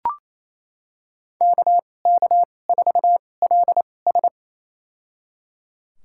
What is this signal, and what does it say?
Morse code, heard as a keyed tone, about 28 wpm, spaced as for 20 wpm, 700 Hz. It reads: KK4LS
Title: USA Call Signs - 5 Characters - 2x Character Spacing - 28wpm